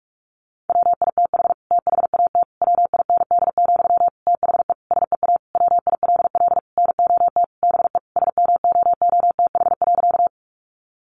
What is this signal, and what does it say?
Morse code, heard as a keyed tone, about 44 wpm, 700 Hz. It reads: WITH THAT WIND, THE SEA WILL NOT BE SMOOTH.
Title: Sentences from Top 1400 Words 45wpm